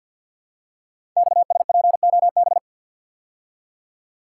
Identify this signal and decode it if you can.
Morse code, heard as a keyed tone, about 48 wpm, 690 Hz. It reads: XSPOB
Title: Code Groups - 5 Letters - 50wpm